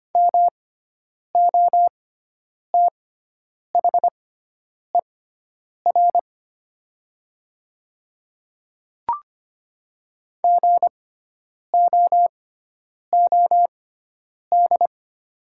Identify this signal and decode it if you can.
Morse code, heard as a keyed tone, about 25 wpm, spaced as for 9 wpm, 700 Hz. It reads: MOTHER GOOD
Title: 6x Character Spacing - Top 200 Words - 25wpm